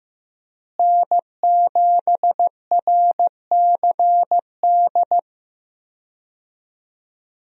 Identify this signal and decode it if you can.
Morse code, heard as a keyed tone, about 16 wpm, 700 Hz. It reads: N7RCD
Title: USA Call Signs - 5 Characters - Repeated 3x - 15wpm